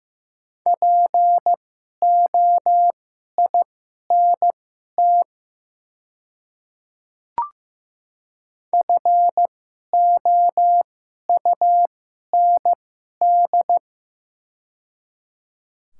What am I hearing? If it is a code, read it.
Morse code, heard as a keyed tone, about 15 wpm, spaced as for 11 wpm, 700 Hz. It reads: POINT FOUND
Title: 2x Character Spacing - Top 500 Words - 15wpm